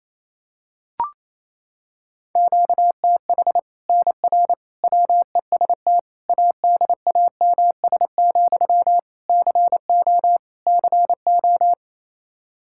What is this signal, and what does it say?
Morse code, heard as a keyed tone, about 28 wpm, 700 Hz. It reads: QTH NR WEST ADAMS, CO CO
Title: Practice QSO - Exchange 1 - Beginning - 28wpm